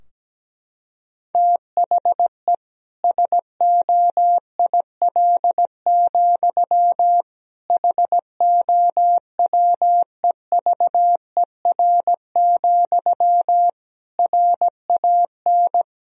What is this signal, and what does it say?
Morse code, heard as a keyed tone, about 17 wpm, 700 Hz. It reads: THE SOIL, HOWEVER, RAN
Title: Sentences from Top 1000 Words - Review Part 4 - 17wpm